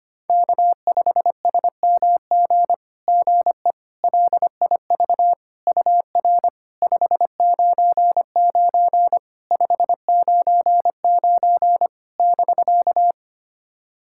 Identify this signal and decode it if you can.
Morse code, heard as a keyed tone, about 25 wpm, 700 Hz. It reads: K5SMG GE LIV UR 599 599 <BK>
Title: SOTA QSO Practice 25wpm